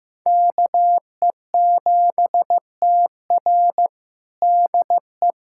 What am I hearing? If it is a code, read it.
Morse code, heard as a keyed tone, about 16 wpm, 700 Hz. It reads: KE7TR DE
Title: USA Callsign DE Callsign - 5 Characters - 15wpm